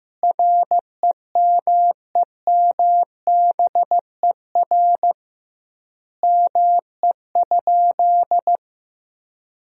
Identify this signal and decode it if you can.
Morse code, heard as a keyed tone, about 15 wpm, 700 Hz. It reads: REMEMBER ME?
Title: 2x Word-Spacing - Sentences from Top 400 Words - 15wpm